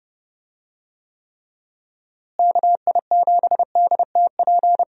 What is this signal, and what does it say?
Morse code, heard as a keyed tone, about 30 wpm, 700 Hz. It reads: KI7DTP
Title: USA Call Signs - 6 Characters - Repeated 3x - 30wpm